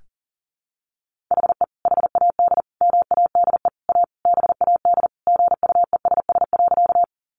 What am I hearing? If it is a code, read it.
Morse code, heard as a keyed tone, about 40 wpm, 700 Hz. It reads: HE HAD MADE A BAD GUESS.
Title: Sentences from Top 1500 Words 40wpm